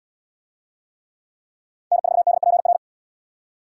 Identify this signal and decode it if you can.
Morse code, heard as a keyed tone, about 72 wpm, 700 Hz. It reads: N4DPR